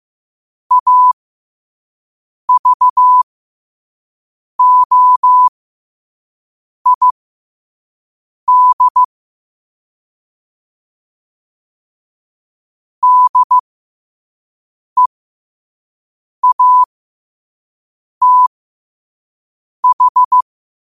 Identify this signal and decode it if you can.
Morse code, heard as a keyed tone, about 15 wpm, spaced as for 5 wpm, 1000 Hz. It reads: AVOID DEATH
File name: qotdfiles / 5wpm